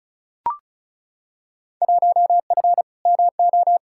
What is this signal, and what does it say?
Morse code, heard as a keyed tone, about 36 wpm, 700 Hz. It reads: 1F MO
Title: Field Day Exchanges Encore - 35wpm